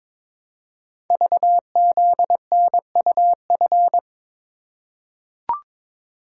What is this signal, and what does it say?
Morse code, heard as a keyed tone, about 22 wpm, 700 Hz. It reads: VZNUF
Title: Code Groups - 5 Letter:Numbers - 22wpm